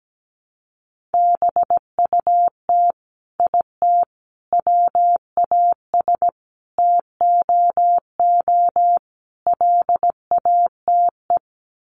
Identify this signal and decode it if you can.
Morse code, heard as a keyed tone, about 17 wpm, 700 Hz. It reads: BUT IT WAS TOO LATE
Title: Sets of 5 Words Encore2 17wpm